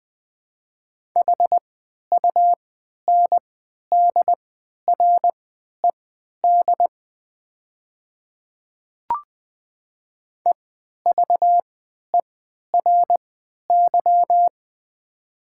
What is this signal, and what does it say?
Morse code, heard as a keyed tone, about 20 wpm, spaced as for 11 wpm, 700 Hz. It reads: HUNDRED EVERY
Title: 3x Character Spacing - Top 500 Words - 20wpm